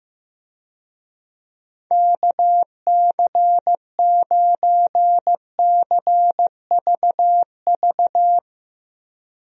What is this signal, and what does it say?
Morse code, heard as a keyed tone, about 15 wpm, 700 Hz. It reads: KC9CVV
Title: USA Call Signs - 6 Characters - Repeated 3x - 15wpm